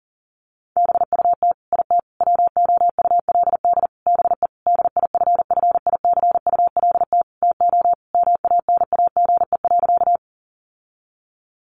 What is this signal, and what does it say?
Morse code, heard as a keyed tone, about 40 wpm, 700 Hz. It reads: BUT IT WOULD BE DIFFICULT TO MANAGE.
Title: Sentences from Top 1400 Words 40wpm